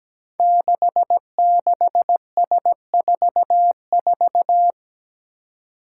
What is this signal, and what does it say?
Morse code, heard as a keyed tone, about 17 wpm, 700 Hz. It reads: 66S44
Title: Five Challenging Letter-Number Combinations 17wpm